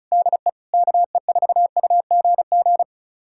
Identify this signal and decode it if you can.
Morse code, heard as a keyed tone, about 34 wpm, 700 Hz. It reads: DE KE4UGG
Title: USA Callsign DE Callsign - 6 Characters - 35wpm